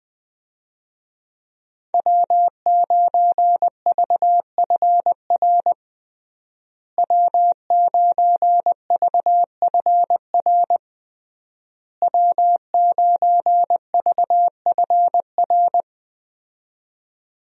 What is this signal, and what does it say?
Morse code, heard as a keyed tone, about 21 wpm, 700 Hz. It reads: W9VFR W9VFR W9VFR
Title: USA Call Signs - 5 Characters - Repeated 3x - 20wpm